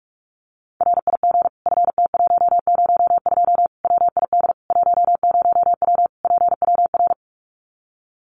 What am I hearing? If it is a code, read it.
Morse code, heard as a keyed tone, about 44 wpm, 700 Hz. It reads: RIG FT102 WID 10W PWR
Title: Common QSO Phrases 45wpm